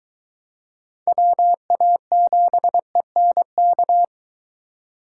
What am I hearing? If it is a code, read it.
Morse code, heard as a keyed tone, about 23 wpm, 700 Hz. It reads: WA7ENK